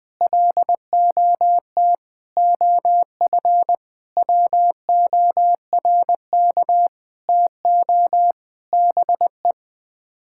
Morse code, heard as a keyed tone, about 21 wpm, 700 Hz. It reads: LOT OF WORK TO BE
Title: Sets of 9 Words - 20wpm